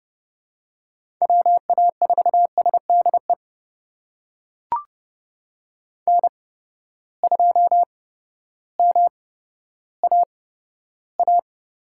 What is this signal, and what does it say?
Morse code, heard as a keyed tone, about 30 wpm, spaced as for 14 wpm, 700 Hz. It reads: WA4SDE N2MAA
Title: USA Call Signs - 8x Character Spacing - 30wpm